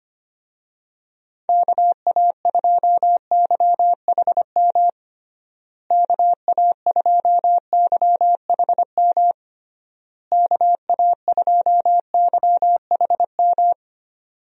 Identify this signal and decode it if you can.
Morse code, heard as a keyed tone, about 25 wpm, 700 Hz. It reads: KA2YHM KA2YHM KA2YHM
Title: USA Call Signs - 6 Characters - Repeated 3x - 25wpm